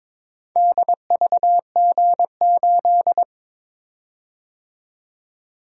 Morse code, heard as a keyed tone, about 22 wpm, 700 Hz. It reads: DVG8
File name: practice / 22wpm / Four Letter-Number Combinations - 22wpm